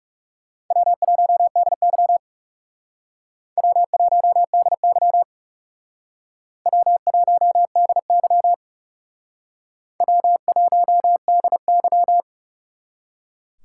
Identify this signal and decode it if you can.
Morse code, heard as a keyed tone, about 39 wpm, 700 Hz. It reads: W1DY W1DY W1DY W1DY